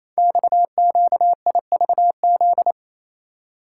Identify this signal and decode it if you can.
Morse code, heard as a keyed tone, about 28 wpm, 700 Hz. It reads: XQIVZ
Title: Code Groups - 5 Letters - 28wpm